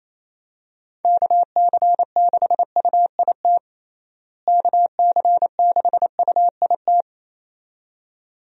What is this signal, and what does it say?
Morse code, heard as a keyed tone, about 28 wpm, 700 Hz. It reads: KC6UIT KC6UIT
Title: USA Call Signs - 6 Characters - Repeated 2x - 28wpm